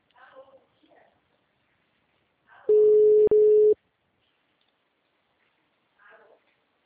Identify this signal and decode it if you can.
Ringback tone (Europe, 425 Hz).